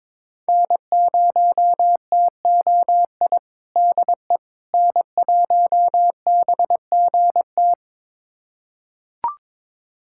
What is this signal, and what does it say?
Morse code, heard as a keyed tone, about 21 wpm, 700 Hz. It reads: N0TOI DE N1BGT
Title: USA Callsign DE Callsign - 5 Characters - 22wpm